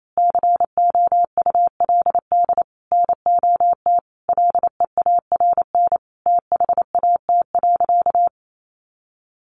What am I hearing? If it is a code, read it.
Morse code, heard as a keyed tone, about 28 wpm, 700 Hz. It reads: COULD NOT LEARN THAT.
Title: Sentences from Top 1200 Words 28wpm